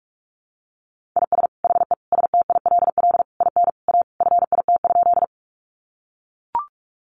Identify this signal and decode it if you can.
Morse code, heard as a keyed tone, about 44 wpm, 700 Hz. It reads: IS HE STILL IN A FIT?
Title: Sentences from Top 800 Words 45wpm